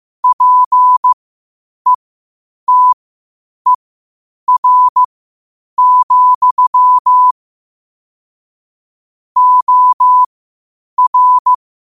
Morse code, heard as a keyed tone, about 15 wpm, spaced as for 8 wpm, 1000 Hz. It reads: PETER, OR
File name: qotdfiles / 8wpm